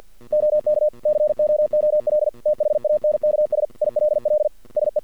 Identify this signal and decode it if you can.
Morse code, heard as a keyed tone, about 48 wpm, 600 Hz. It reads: OK COOL THANKS ILL SE
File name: archived / CRSnet